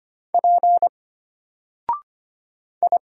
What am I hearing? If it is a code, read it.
Morse code, heard as a keyed tone, about 25 wpm, 700 Hz. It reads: P I
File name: Single-Letter-Number-25wpm